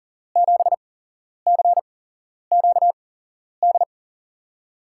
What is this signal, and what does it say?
Morse code, heard as a keyed tone, about 42 wpm, spaced as for 11 wpm, 700 Hz. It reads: 7CQD
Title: USA Call Signs - 6 Characters - 8x Character Spacing - 40wpm